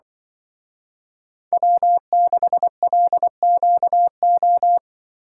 Morse code, heard as a keyed tone, about 23 wpm, 700 Hz. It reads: W6LQO
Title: Speed Racing - USA Call Signs 20wpm